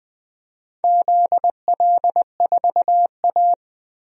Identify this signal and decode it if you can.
Morse code, heard as a keyed tone, about 20 wpm, 700 Hz. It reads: ZL4A
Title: DXCC-100 Call Signs - 4 Characters - 20wpm